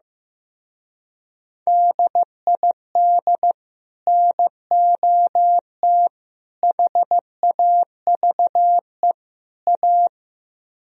Morse code, heard as a keyed tone, about 15 wpm, 700 Hz. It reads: DID NOT HAVE A
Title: Speed-Racing - Sets of 4 Words 15wpm